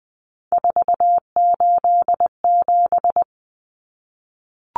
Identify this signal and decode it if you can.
Morse code, heard as a keyed tone, about 20 wpm, 700 Hz. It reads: 487
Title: Numbers - 3-Digit 20wpm